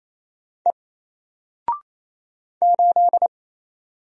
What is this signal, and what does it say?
Morse code, heard as a keyed tone, about 28 wpm, 700 Hz. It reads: E 8
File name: Single-Letter-Number-28wpm